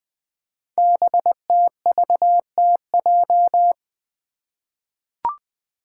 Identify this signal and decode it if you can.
Morse code, heard as a keyed tone, about 20 wpm, 700 Hz. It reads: BTVTJ
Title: Code Groups - 5 Letters - 20wpm